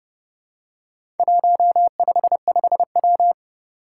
Morse code, heard as a keyed tone, about 31 wpm, 700 Hz. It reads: 155W
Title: Four Challenging Letter-Number Combinations 30wpm